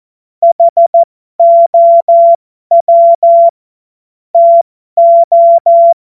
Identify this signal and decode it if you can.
Morse code, heard as a keyed tone, about 14 wpm, spaced as for 12 wpm, 680 Hz. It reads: HOW TO